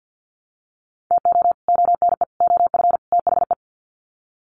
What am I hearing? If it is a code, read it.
Morse code, heard as a keyed tone, about 48 wpm, 700 Hz. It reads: TO ONE OF THE
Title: Sets of 4 Words 50wpm